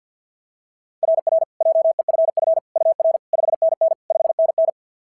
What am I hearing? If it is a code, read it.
Morse code, heard as a keyed tone, about 52 wpm, 690 Hz. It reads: RR JEFF UR 5NN 5NN
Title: SOTA QSO Practice 50wpm